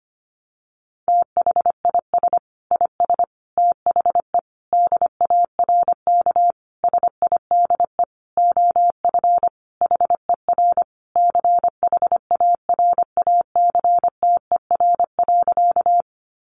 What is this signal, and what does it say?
Morse code, heard as a keyed tone, about 25 wpm, 700 Hz. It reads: THIS IS THE DARK SIDE OF HER CHARACTER.
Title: Sentences from Top 1200 Words 25wpm